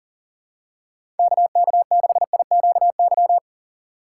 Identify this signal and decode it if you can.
Morse code, heard as a keyed tone, about 40 wpm, 700 Hz. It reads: KK6IQY
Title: USA Call Signs - 6 Characters - 4x Character Spacing - 40wpm